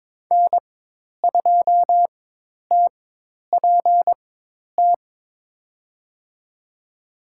Morse code, heard as a keyed tone, about 22 wpm, spaced as for 10 wpm, 700 Hz. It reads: N2TPT
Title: USA Call Signs - 5 Characters - 4x Character Spacing - 22wpm